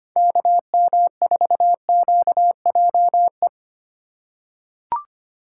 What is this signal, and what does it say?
Morse code, heard as a keyed tone, about 25 wpm, 700 Hz. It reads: KM4QJE
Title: USA Call Signs - 6 Characters - 3x Character Spacing - 25wpm